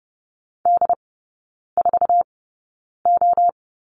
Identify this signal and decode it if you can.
Morse code, heard as a keyed tone, about 30 wpm, spaced as for 9 wpm, 700 Hz. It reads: D4O
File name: course / 30:10wpm / 087 Lesson 15e - Number 5 - Three Characters 30wpm